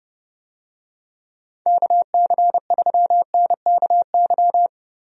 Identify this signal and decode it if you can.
Morse code, heard as a keyed tone, about 30 wpm, 700 Hz. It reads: KC3NKY
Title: USA Call Signs - 6 Characters - Repeated 3x - 30wpm